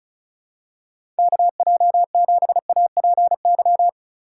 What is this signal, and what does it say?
Morse code, heard as a keyed tone, about 36 wpm, 700 Hz. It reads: KJ7APY
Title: USA Call Signs - 6 Characters - 8x Character Spacing - 35wpm